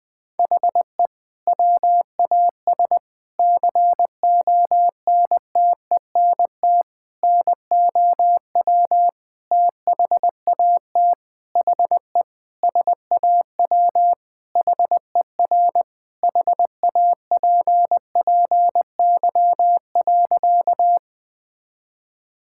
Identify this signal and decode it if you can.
Morse code, heard as a keyed tone, about 20 wpm, 700 Hz. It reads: HE WAS CONTENT NOW THAT HE SAW HER HAPPY.
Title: Sentences from Top 1000 Words - Review Part 6 - 20wpm